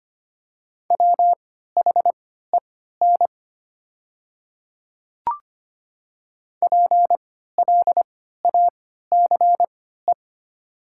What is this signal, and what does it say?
Morse code, heard as a keyed tone, about 25 wpm, spaced as for 14 wpm, 700 Hz. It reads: WHEN PLACE
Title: 3x Character Spacing - Top 300 Words - 25wpm